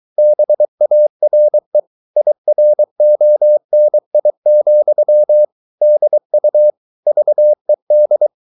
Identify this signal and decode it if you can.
Morse code, heard as a keyed tone, about 23 wpm, 600 Hz. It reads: BARE IRONI, DU VED
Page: Ferie 23 wpm